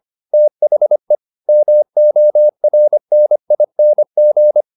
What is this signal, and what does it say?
Morse code, heard as a keyed tone, about 25 wpm, 600 Hz. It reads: THE MORNING
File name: the old man and the sea part 1 25wpm